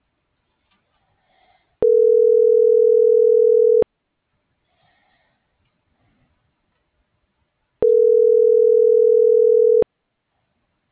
Ringback tone (North America), 440 + 480 Hz.